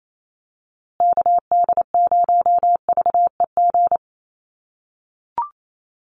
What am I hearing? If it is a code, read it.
Morse code, heard as a keyed tone, about 28 wpm, 700 Hz. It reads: KD0VEG
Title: USA Call Signs Repeated 2x 28wpm